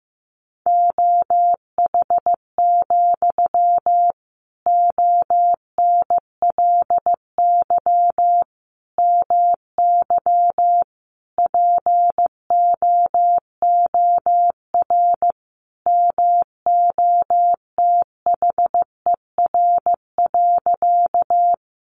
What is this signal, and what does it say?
Morse code, heard as a keyed tone, about 15 wpm, 700 Hz. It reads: OH, ONLY MY POOR MOTHER.
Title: Sentences from Top 1000 Words 15wpm